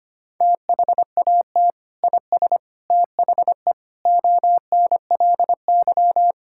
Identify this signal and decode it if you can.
Morse code, heard as a keyed tone, about 25 wpm, 700 Hz. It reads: THAT IS THE ONLY
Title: Sentences from Top 1000 Words - Review Part 5 - 25wpm